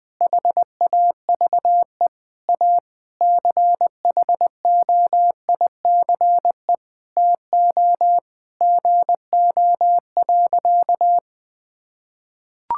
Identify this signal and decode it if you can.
Morse code, heard as a keyed tone, about 21 wpm, 700 Hz. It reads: HAVE A CHOICE TO GO.
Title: Sentences with 4-Word Phrases - 20wpm